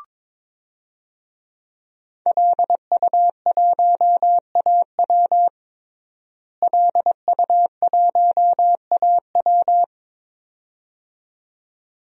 Morse code, heard as a keyed tone, about 22 wpm, 700 Hz. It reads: LU1AW LU1AW
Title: DXCC-100 Call Signs - 5 Characters - Repeated 2x - 22wpm